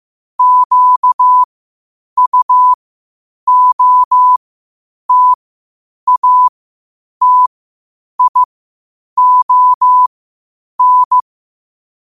Morse code, heard as a keyed tone, about 15 wpm, spaced as for 8 wpm, 1000 Hz. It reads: QUOTATION
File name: qotdfiles / 8wpm